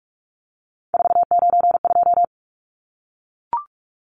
Morse code, heard as a keyed tone, about 44 wpm, 700 Hz. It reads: 492
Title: Numbers - 3-Digit 45wpm